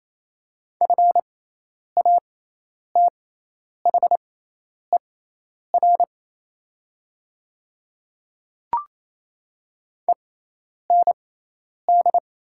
Morse code, heard as a keyed tone, about 27 wpm, spaced as for 10 wpm, 700 Hz. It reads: FATHER END
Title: 6x Character Spacing - Top 300 Words - 28wpm